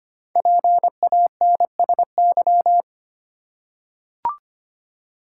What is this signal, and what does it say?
Morse code, heard as a keyed tone, about 25 wpm, 700 Hz. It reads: PANSY
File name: practice / 25wpm / Difficult Words with Hints - 25wpm